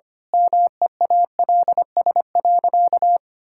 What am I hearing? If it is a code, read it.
Morse code, heard as a keyed tone, about 25 wpm, 700 Hz. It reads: MEALS.
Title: Pangram Challenge v2 - 25wpm